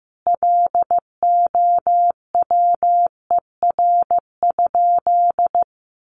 Morse code, heard as a keyed tone, about 15 wpm, 700 Hz. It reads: LOWER?
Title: Sentences from Top 700 Words 15wpm